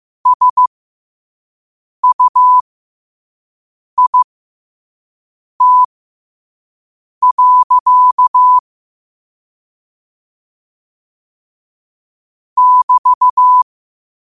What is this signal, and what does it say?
Morse code, heard as a keyed tone, about 15 wpm, spaced as for 5 wpm, 1000 Hz. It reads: SUIT. <BT>
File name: qotdfiles / 5wpm